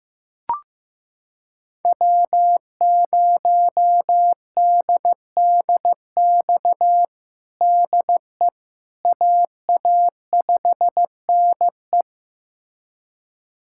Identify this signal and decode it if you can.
Morse code, heard as a keyed tone, about 15 wpm, 700 Hz. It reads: W0DDX DE AA5NE
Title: USA Callsign DE Callsign - 5 Characters - 15wpm